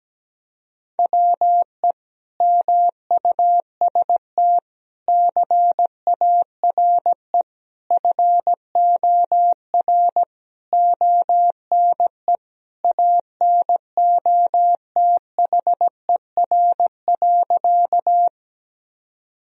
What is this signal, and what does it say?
Morse code, heard as a keyed tone, about 17 wpm, 700 Hz. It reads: WE MUST CARE FOR ONE ANOTHER.
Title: Sentences from Top 1000 Words - Review Part 1 - 17wpm